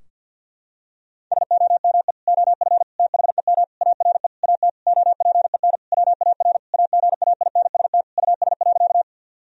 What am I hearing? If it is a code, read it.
Morse code, heard as a keyed tone, about 52 wpm, 700 Hz. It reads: SOME OF THEM ARE AT OPEN WAR AGAINST US.
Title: Sentences from Top 1000 Words - Review Part 2 - 50wpm